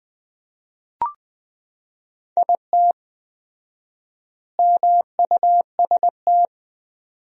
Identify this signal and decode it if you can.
Morse code, heard as a keyed tone, about 20 wpm, 700 Hz. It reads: IT MUST
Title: 4x Word-Spacing - Sentences from Top 300 Words - 20wpm